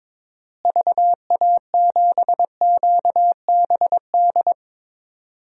Morse code, heard as a keyed tone, about 22 wpm, 700 Hz. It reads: VA7QBD
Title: DXCC-100 Call Signs - 6 Characters - Repeated 3x - 22wpm